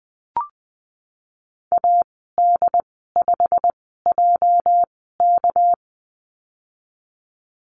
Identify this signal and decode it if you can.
Morse code, heard as a keyed tone, about 20 wpm, spaced as for 15 wpm, 700 Hz. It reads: AD5JK
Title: 231 Lesson 37g - Letter x - Call Signs 20wpm